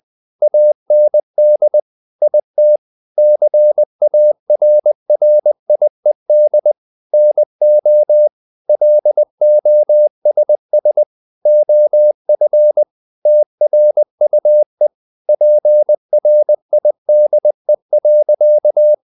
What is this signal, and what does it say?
Morse code, heard as a keyed tone, about 20 wpm, 600 Hz. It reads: AND IT CARRIED NO LOSS OF TRUE PRIDE.
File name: the old man and the sea / the old man and the sea part 1 20wpm